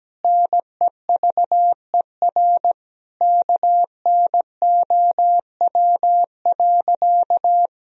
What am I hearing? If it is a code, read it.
Morse code, heard as a keyed tone, about 17 wpm, 700 Hz. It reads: NEVER KNOW.
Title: Sentences from Top 300 Words and Abbreviations 17wpm